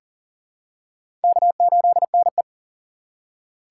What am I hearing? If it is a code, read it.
Morse code, heard as a keyed tone, about 40 wpm, 700 Hz. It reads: K8NE
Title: USA Call Signs - 4 Characters - 2x Character Spacing - 40wpm